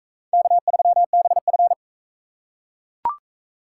Morse code, heard as a keyed tone, about 41 wpm, 700 Hz. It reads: K3BF